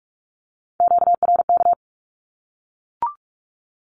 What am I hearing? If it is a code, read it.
Morse code, heard as a keyed tone, about 44 wpm, 700 Hz. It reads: QRK
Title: Q-Codes 45wpm